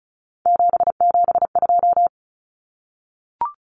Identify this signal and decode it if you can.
Morse code, heard as a keyed tone, about 34 wpm, 700 Hz. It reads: 772
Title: Numbers - 3-Digit 35wpm